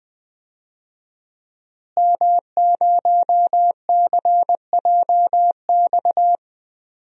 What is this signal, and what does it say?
Morse code, heard as a keyed tone, about 20 wpm, 700 Hz. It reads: M0CJX
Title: DXCC-100 Call Signs - 5 Characters - Repeated 3x - 20wpm